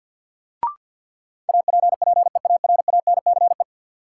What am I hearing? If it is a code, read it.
Morse code, heard as a keyed tone, about 52 wpm, 700 Hz. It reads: APPEARANCE
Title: Sentences from Top 1100 Words - New Word Drill Alt - 50wpm